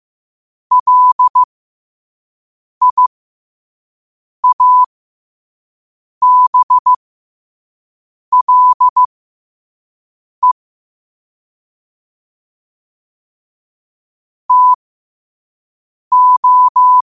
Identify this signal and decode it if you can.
Morse code, heard as a keyed tone, about 15 wpm, spaced as for 5 wpm, 1000 Hz. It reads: LIABLE TO
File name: qotdfiles / 5wpm